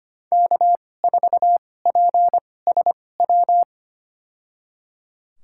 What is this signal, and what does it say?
Morse code, heard as a keyed tone, about 25 wpm, spaced as for 18 wpm, 700 Hz. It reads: K4PSW
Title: USA Call Signs - 2x Character Spacing - 25wpm